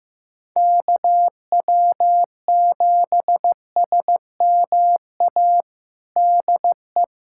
Morse code, heard as a keyed tone, about 15 wpm, 700 Hz. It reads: KW7SMA DE
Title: USA Callsign DE Callsign 15wpm